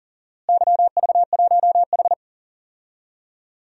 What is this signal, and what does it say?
Morse code, heard as a keyed tone, about 40 wpm, 700 Hz. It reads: YV1H
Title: DXCC-100 Call Signs - 4 Characters - Repeated 2x - 40wpm